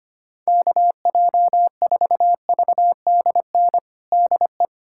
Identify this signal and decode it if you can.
Morse code, heard as a keyed tone, about 25 wpm, 700 Hz. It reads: KJ4VDN DE
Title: USA Callsign DE Callsign - 6 Characters - 25wpm